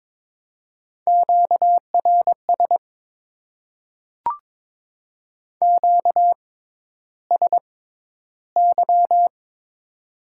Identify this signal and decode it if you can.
Morse code, heard as a keyed tone, about 22 wpm, spaced as for 11 wpm, 700 Hz. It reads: QRS QSY
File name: Top 90 QSO Elements - 6x Character Spacing - 22wpm